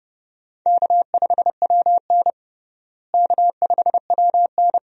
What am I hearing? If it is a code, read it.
Morse code, heard as a keyed tone, about 30 wpm, 700 Hz. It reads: K5WN K5WN
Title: USA Call Signs - 4 Characters - Repeated 3x - 30wpm